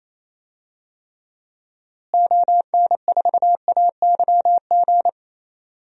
Morse code, heard as a keyed tone, about 28 wpm, 700 Hz. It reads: ON4AYG
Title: DXCC-100 Call Signs - 6 Characters - Repeated 2x - 28wpm